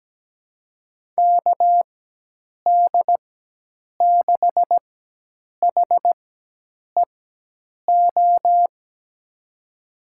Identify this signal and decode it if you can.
Morse code, heard as a keyed tone, about 17 wpm, spaced as for 8 wpm, 700 Hz. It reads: KD6HEO
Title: USA Call Signs - 6 Characters - 4x Character Spacing - 17wpm